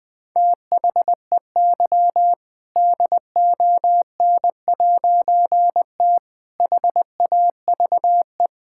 Morse code, heard as a keyed tone, about 20 wpm, 700 Hz. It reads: THEY DON'T HAVE
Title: All-Age Appropriate Jokes - 20wpm